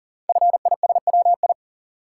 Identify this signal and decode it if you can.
Morse code, heard as a keyed tone, about 38 wpm, 690 Hz. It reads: FISWI
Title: Code Groups - 5 Letters - 40wpm